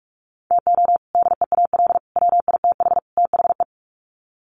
Morse code, heard as a keyed tone, about 44 wpm, 700 Hz. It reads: TO DEAL WITH THE
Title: Sets of 4 Words 45wpm